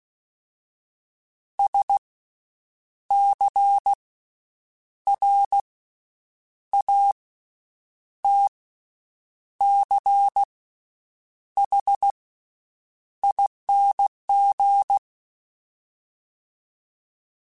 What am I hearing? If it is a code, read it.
Morse code, heard as a keyed tone, about 16 wpm, spaced as for 7 wpm, 800 Hz. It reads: SCRATCHING
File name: Special INGs / Special INGs-mp3s / VerySlowToFaster Ing @15